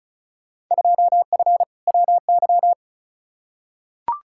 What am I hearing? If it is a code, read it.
Morse code, heard as a keyed tone, about 36 wpm, 700 Hz. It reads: 2F WY
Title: Field Day Exchanges Encore - 35wpm